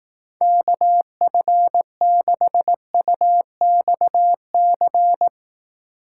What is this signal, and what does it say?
Morse code, heard as a keyed tone, about 18 wpm, 700 Hz. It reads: KF6UXC
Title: Speed Racing - USA Call Signs 15wpm